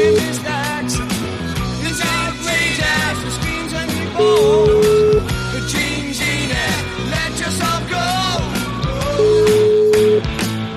Ringback tone (Europe, 425 Hz), with noise.